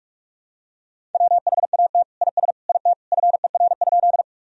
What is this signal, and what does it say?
Morse code, heard as a keyed tone, about 41 wpm, 700 Hz. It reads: WHAT IS IT FER?
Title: Sentences from Top 100 Words and Abbreviations 45wpm